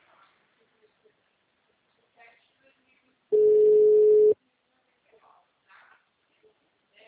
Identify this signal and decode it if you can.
Ringback tone (Europe, 425 Hz), with speech.